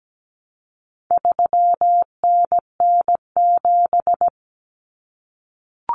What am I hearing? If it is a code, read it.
Morse code, heard as a keyed tone, about 17 wpm, 700 Hz. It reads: 3NN7
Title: Common Cut Numbers - 4-Digit 17wpm